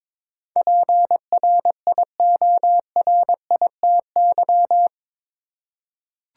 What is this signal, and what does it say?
Morse code, heard as a keyed tone, about 22 wpm, 700 Hz. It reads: PRIORITY
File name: practice / 22wpm / Eight Letter Words - 22wpm